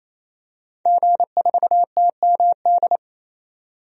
Morse code, heard as a keyed tone, about 28 wpm, 700 Hz. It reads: G4TMD
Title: DXCC-100 Call Signs - 5 Characters - 28wpm